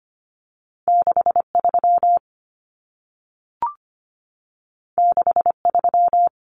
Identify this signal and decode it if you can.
Morse code, heard as a keyed tone, about 25 wpm, 700 Hz. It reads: 63 63
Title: Numbers - 2-Digit 25wpm